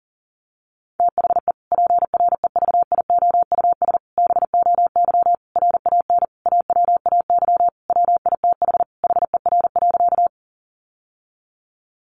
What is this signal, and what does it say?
Morse code, heard as a keyed tone, about 40 wpm, 700 Hz. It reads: THE PREVIOUS BOY RAN AWAY WITH HER.
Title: Sentences from Top 1500 Words 40wpm